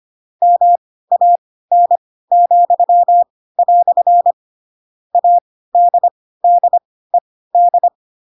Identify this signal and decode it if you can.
Morse code, heard as a keyed tone, about 25 wpm, spaced as for 16 wpm, 700 Hz. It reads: MAN," ADDED
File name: Pinocchio / Pinocchio ch3 10 25 ik1hge